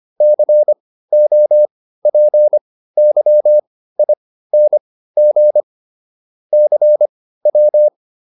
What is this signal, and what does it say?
Morse code, heard as a keyed tone, about 25 wpm, spaced as for 15 wpm, 600 Hz. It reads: COPYING CW
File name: SS203 15